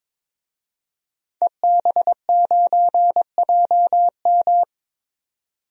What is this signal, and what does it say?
Morse code, heard as a keyed tone, about 22 wpm, 700 Hz. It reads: EB9JM
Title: DXCC-100 Call Signs - 5 Characters - 22wpm